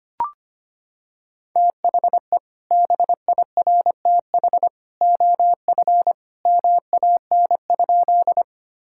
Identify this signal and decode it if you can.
Morse code, heard as a keyed tone, about 25 wpm, 700 Hz. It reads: THE BIRTH OF MAN?
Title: Sentences from Top 1000 Words - Review Part 5 - 25wpm